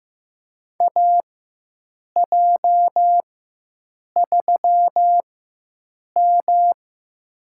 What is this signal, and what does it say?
Morse code, heard as a keyed tone, about 15 wpm, spaced as for 7 wpm, 700 Hz. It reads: AJ3M
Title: USA Call Signs - 4 Characters - 4x Character Spacing - 15wpm